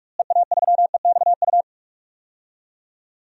Morse code, heard as a keyed tone, about 46 wpm, 700 Hz. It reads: EA3EXU
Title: DXCC-100 Call Signs - 6 Characters - Repeated 2x - 45wpm